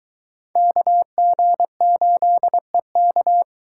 Morse code, heard as a keyed tone, about 22 wpm, 700 Hz. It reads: KG8EK